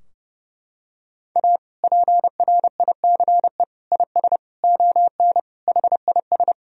Morse code, heard as a keyed tone, about 32 wpm, 700 Hz. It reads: A PRICE IS ON HIS
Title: Sentences from Top 1000 Words - Review Part 4 - 30wpm